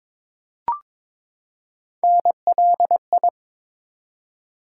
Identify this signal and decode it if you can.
Morse code, heard as a keyed tone, about 22 wpm, 700 Hz. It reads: NLI
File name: ARRL Sections v2 - 22wpm